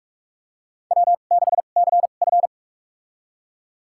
Morse code, heard as a keyed tone, about 46 wpm, spaced as for 32 wpm, 700 Hz. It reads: W6CF
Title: USA Call Signs - 4 Characters - 2x Character Spacing - 45wpm